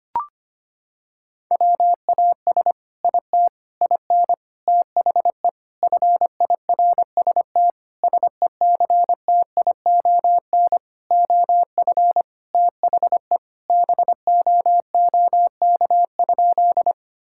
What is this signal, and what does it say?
Morse code, heard as a keyed tone, about 25 wpm, 700 Hz. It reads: WAS IT IN THE FIRST SECTION OF THE BOOK?
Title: Sentences from Top 1000 Words - Review Part 5 - 25wpm